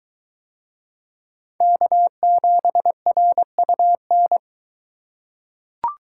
Morse code, heard as a keyed tone, about 22 wpm, 700 Hz. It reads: K7RUN